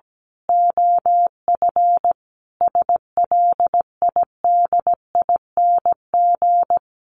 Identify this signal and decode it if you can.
Morse code, heard as a keyed tone, about 17 wpm, 700 Hz. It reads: OF SLIDING
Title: A Sound Of Thunder 17wpm